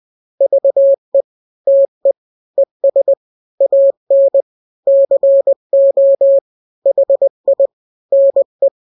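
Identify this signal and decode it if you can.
Morse code, heard as a keyed tone, about 20 wpm, 550 Hz. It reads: VE TE ES AN CO HI NE
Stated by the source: Morse drill of two-letter n-grams at 20/18 WPM (Farnsworth), from KK7RR.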